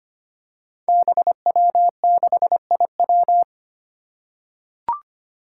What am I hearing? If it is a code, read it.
Morse code, heard as a keyed tone, about 25 wpm, 700 Hz. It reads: BW6IW
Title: Five Challenging Letter-Number Combinations 25wpm